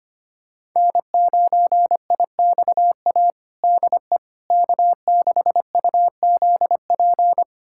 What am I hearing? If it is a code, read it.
Morse code, heard as a keyed tone, about 25 wpm, 700 Hz. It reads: N9IXA DE K6UZP
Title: USA Callsign DE Callsign - 5 Characters - 25wpm